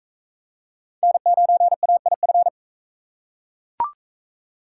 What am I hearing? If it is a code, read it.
Morse code, heard as a keyed tone, about 38 wpm, 700 Hz. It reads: N9AIF